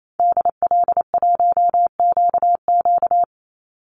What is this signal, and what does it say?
Morse code, heard as a keyed tone, about 28 wpm, 700 Hz. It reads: DL1QQ
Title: Call Signs Repeated 3x - 28wpm